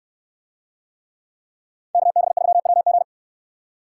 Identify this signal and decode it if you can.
Morse code, heard as a keyed tone, about 68 wpm, 700 Hz. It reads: KB3WG